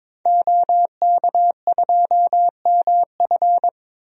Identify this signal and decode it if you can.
Morse code, heard as a keyed tone, about 21 wpm, 700 Hz. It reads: OK2MF
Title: DXCC-100 Call Signs - 5 Characters - 22wpm